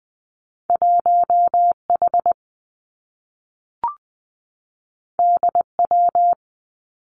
Morse code, heard as a keyed tone, about 20 wpm, 700 Hz. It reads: 1 H D W
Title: Two Challenging Letter-Number Combinations 20wpm